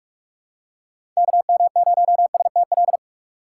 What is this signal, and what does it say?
Morse code, heard as a keyed tone, about 46 wpm, 700 Hz. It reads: KM0STL
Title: USA Call Signs - 6x Character Spacing - 45wpm